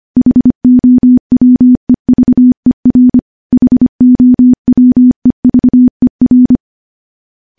Morse code, heard as a keyed tone, about 25 wpm, 250 Hz. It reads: HOWEVER HOWEVER
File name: morse / morse-training-25WPM